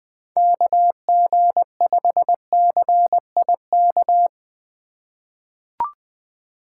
Morse code, heard as a keyed tone, about 20 wpm, 700 Hz. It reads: KG5CIK
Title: 2023 POTA Hunter Call Signs - 20wpm